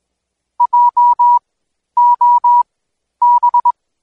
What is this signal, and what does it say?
Morse code, heard as a keyed tone, about 21 wpm, spaced as for 11 wpm, 970 Hz. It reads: JOB
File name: Code Quick / AudioDownld2 / Seg120-Day 25-DayDrill 25